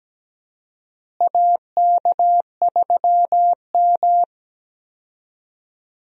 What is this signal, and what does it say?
Morse code, heard as a keyed tone, about 17 wpm, 700 Hz. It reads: AK3M
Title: USA Call Signs - 4 Characters - Repeated 2x - 17wpm